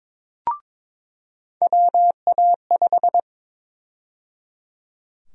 Morse code, heard as a keyed tone, about 22 wpm, 700 Hz. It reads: WA5
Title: USA Call Sign Prefixes - 22wpm